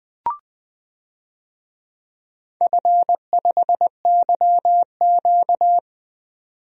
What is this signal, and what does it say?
Morse code, heard as a keyed tone, about 20 wpm, 700 Hz. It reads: F5YQ
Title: DXCC-100 Call Signs - 4 Characters - 20wpm